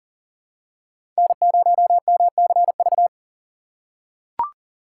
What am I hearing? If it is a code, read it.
Morse code, heard as a keyed tone, about 40 wpm, 700 Hz. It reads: N0MCV
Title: USA Call Signs - 5 Characters - 4x Character Spacing - 40wpm